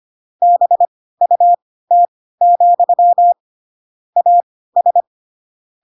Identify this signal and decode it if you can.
Morse code, heard as a keyed tone, about 25 wpm, spaced as for 16 wpm, 700 Hz. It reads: BUT, AS